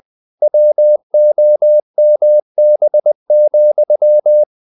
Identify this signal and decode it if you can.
Morse code, heard as a keyed tone, about 20 wpm, 600 Hz. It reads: WOMB,
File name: the old man and the sea / the old man and the sea part 1 20wpm